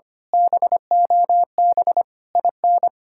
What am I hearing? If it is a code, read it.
Morse code, heard as a keyed tone, about 24 wpm, 700 Hz. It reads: BOB IN
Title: Speed-Racing Common QSO Phrases - 25wpm